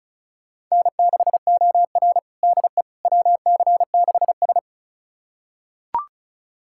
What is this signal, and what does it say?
Morse code, heard as a keyed tone, about 34 wpm, 700 Hz. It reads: N6OR DE WC6S
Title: USA Callsign DE Callsign - 4 Characters - 35wpm